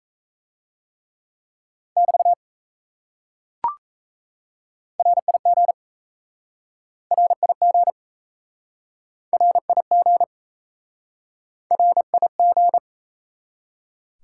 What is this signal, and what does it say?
Morse code, heard as a keyed tone, about 37 wpm, 700 Hz. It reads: <BT> RIG RIG RIG RIG